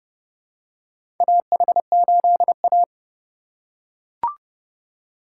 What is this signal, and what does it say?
Morse code, heard as a keyed tone, about 31 wpm, 700 Hz. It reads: AH8A
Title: USA Call Signs - 4 Characters - 4x Character Spacing - 30wpm